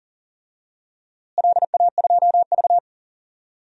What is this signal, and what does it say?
Morse code, heard as a keyed tone, about 40 wpm, 700 Hz. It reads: LA2V
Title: DXCC-100 Call Signs - 4 Characters - Repeated 3x - 40wpm